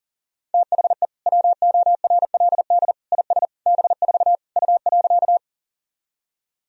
Morse code, heard as a keyed tone, about 40 wpm, 700 Hz. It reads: THE WORLD IS B4 U.
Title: Sentences from Top 200 Words and Abbreviations 40wpm